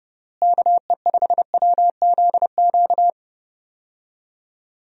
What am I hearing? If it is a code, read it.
Morse code, heard as a keyed tone, about 32 wpm, 700 Hz. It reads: KE5WZQ
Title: USA Call Signs - 6 Characters - Repeated 2x - 30wpm